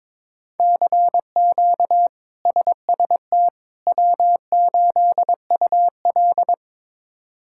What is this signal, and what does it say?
Morse code, heard as a keyed tone, about 22 wpm, 700 Hz. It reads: CQ SST W8UL
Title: K1USN SST QSO Practice - 22wpm